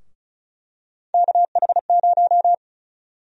Morse code, heard as a keyed tone, about 36 wpm, 700 Hz. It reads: KH0
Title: USA Call Sign Prefixes - 35wpm